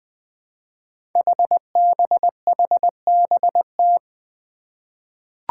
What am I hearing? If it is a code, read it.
Morse code, heard as a keyed tone, about 20 wpm, 700 Hz. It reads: HBHBT
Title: Five Challenging Letter-Number Combinations 20wpm